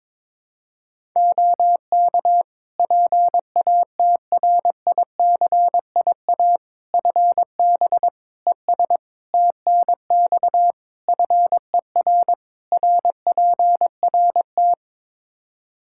Morse code, heard as a keyed tone, about 23 wpm, 700 Hz. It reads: OK PATRICIA FB ES TNX FER RPRT
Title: Practice QSO - Exchange 2 - Info - 22wpm